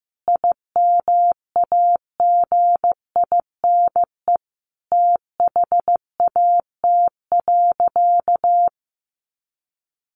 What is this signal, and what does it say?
Morse code, heard as a keyed tone, about 15 wpm, 700 Hz. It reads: IMAGINE THAT.
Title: Sentences from Top 1400 Words 15wpm